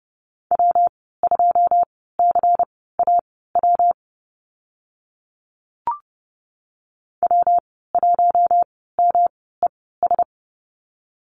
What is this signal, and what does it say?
Morse code, heard as a keyed tone, about 30 wpm, spaced as for 17 wpm, 700 Hz. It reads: W2CAW W1MES
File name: Lesson 19g - Letter w - Call Signs 30wpm